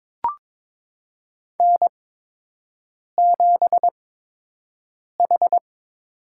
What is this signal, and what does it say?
Morse code, heard as a keyed tone, about 22 wpm, spaced as for 6 wpm, 700 Hz. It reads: N7H
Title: USA Call Signs - 5 Characters - 8x Character Spacing - 22wpm